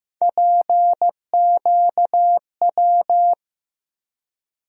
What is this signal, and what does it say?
Morse code, heard as a keyed tone, about 15 wpm, 700 Hz. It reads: PQW
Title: USA Call Sign Postfixes - 3 Letters 15wpm